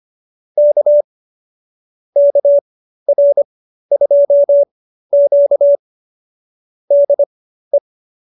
Morse code, heard as a keyed tone, about 25 wpm, spaced as for 13 wpm, 590 Hz. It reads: K KR2Q DE